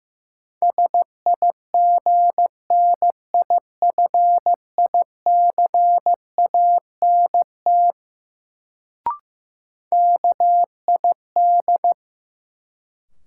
Morse code, heard as a keyed tone, about 16 wpm, 700 Hz. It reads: SIGNIFICANT KID
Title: Sentences from Top 1600 Words - New Word Drill Alt - 15wpm